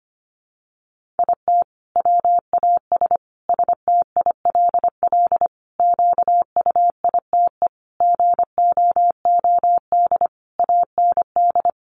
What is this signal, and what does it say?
Morse code, heard as a keyed tone, about 25 wpm, 700 Hz. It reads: IT WAS STILL QUITE GOOD AND
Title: Sentences from Top 1500 Words 25wpm